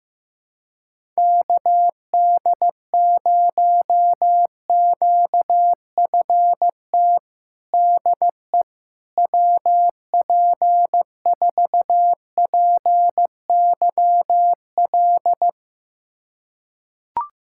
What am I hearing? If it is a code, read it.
Morse code, heard as a keyed tone, about 15 wpm, 700 Hz. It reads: KD0QFT DE WP4PYL
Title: USA Callsign DE Callsign - 6 Characters - 15wpm